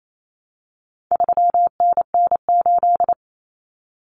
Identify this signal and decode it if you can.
Morse code, heard as a keyed tone, about 28 wpm, 700 Hz. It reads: 3NN8
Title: Common Cut Numbers - 4-Digit 28wpm